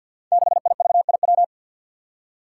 Morse code, heard as a keyed tone, about 52 wpm, 700 Hz. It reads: 6IVIW
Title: Five Challenging Letter-Number Combinations 50wpm